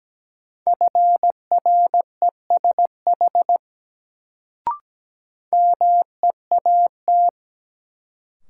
Morse code, heard as a keyed tone, about 17 wpm, 700 Hz. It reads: FRESH MEAT
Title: Sentences from Top 1200 Words - New Word Drill Alt - 17wpm